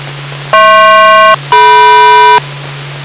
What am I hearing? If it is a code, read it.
Aircraft SELCAL, heard as two pairs of tones. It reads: HP-DM